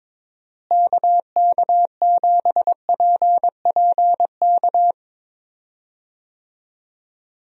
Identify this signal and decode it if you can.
Morse code, heard as a keyed tone, about 22 wpm, 700 Hz. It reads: KK7PPK
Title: USA Call Signs - 6 Characters - Repeated 2x - 22wpm